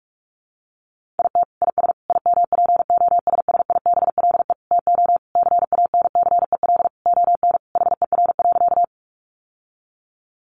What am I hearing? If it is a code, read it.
Morse code, heard as a keyed tone, about 44 wpm, 700 Hz. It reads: IT IS IMPOSSIBLE TO CANCEL ON HER.
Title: Sentences from Top 1200 Words 45wpm